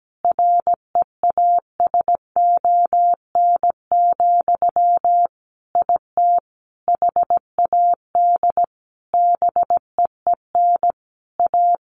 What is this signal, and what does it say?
Morse code, heard as a keyed tone, about 17 wpm, 700 Hz. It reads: REASON, IT HAD BEEN A
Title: Sentences from Top 1500 Words 17wpm